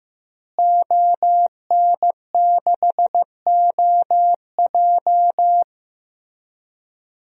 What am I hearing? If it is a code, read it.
Morse code, heard as a keyed tone, about 15 wpm, 700 Hz. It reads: ON6OJ
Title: DXCC-100 Call Signs - 5 Characters - Repeated 3x - 15wpm